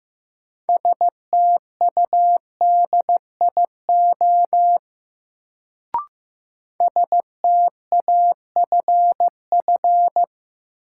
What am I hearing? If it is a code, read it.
Morse code, heard as a keyed tone, about 15 wpm, 700 Hz. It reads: STUDIO STAFF
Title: Sentences from Top 1200 Words - New Word Drill - 15wpm